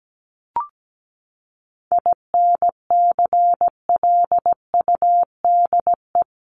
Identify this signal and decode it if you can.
Morse code, heard as a keyed tone, about 17 wpm, 700 Hz. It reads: INCLUDE
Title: ED Suffix - Top 100 Words with Base Words 17wpm